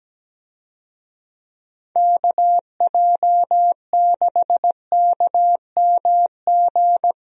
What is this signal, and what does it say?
Morse code, heard as a keyed tone, about 17 wpm, 700 Hz. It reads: KJ6KMG
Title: USA Call Signs - 6 Characters - Repeated 2x - 17wpm